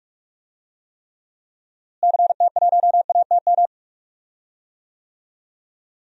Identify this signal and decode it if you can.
Morse code, heard as a keyed tone, about 48 wpm, 700 Hz. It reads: CT1ATM
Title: DXCC-100 Call Signs - 6 Characters - Repeated 2x - 45wpm